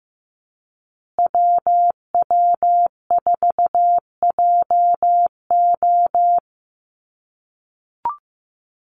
Morse code, heard as a keyed tone, about 15 wpm, 700 Hz. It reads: WW4JO
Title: USA Call Signs Repeated 2x 15wpm